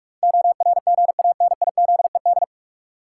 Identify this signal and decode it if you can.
Morse code, heard as a keyed tone, about 47 wpm, 700 Hz. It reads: ORGANIZED
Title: Sentences from Top 1300 Words - New Word Drill - 45wpm